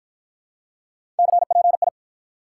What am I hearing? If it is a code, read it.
Morse code, heard as a keyed tone, about 52 wpm, 700 Hz. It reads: CPI